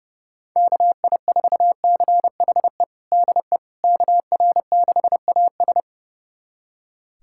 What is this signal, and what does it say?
Morse code, heard as a keyed tone, about 32 wpm, 690 Hz. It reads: KI4CHE DE KR6AS
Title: USA Callsign DE Callsign 30wpm